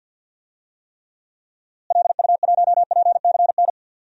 Morse code, heard as a keyed tone, about 45 wpm, 700 Hz. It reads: LU1PCN
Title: DXCC-100 Call Signs - 6 Characters - Repeated 3x - 50wpm